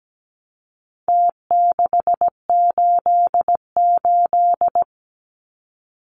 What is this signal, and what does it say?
Morse code, heard as a keyed tone, about 17 wpm, 700 Hz. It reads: T688
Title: Numbers - Contest Style - 4-Digit 17wpm